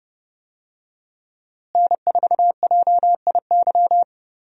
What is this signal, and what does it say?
Morse code, heard as a keyed tone, about 30 wpm, 700 Hz. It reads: N4JIY
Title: USA Call Signs - 5 Characters - Repeated 3x - 30wpm